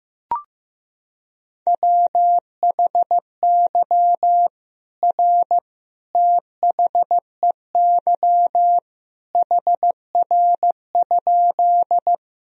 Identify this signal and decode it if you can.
Morse code, heard as a keyed tone, about 15 wpm, 700 Hz. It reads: WHY R THEY HR?
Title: Sentences from Top 200 Words and Abbreviations 15wpm